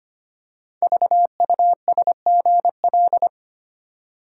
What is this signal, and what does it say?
Morse code, heard as a keyed tone, about 25 wpm, 690 Hz. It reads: VUSGL
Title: Code Groups - 5 Letters - 25wpm